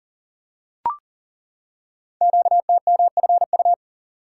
Morse code, heard as a keyed tone, about 38 wpm, 700 Hz. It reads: QTMFU
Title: Code Groups - 5 Letters - 40wpm